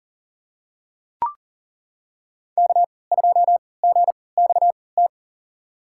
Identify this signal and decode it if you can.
Morse code, heard as a keyed tone, about 38 wpm, spaced as for 22 wpm, 700 Hz. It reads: K2GXT
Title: USA Call Signs - 5 Characters - 3x Character Spacing - 40wpm